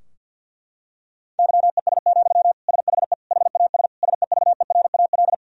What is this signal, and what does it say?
Morse code, heard as a keyed tone, about 52 wpm, 700 Hz. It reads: YES, SHE HAS SEVERAL
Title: Sentences from Top 1000 Words - Review Part 2 - 50wpm